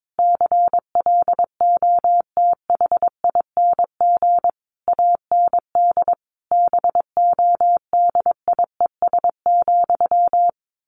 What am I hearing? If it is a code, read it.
Morse code, heard as a keyed tone, about 22 wpm, 700 Hz. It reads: CLOTHING AND BODIES,
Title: A Sound Of Thunder 22wpm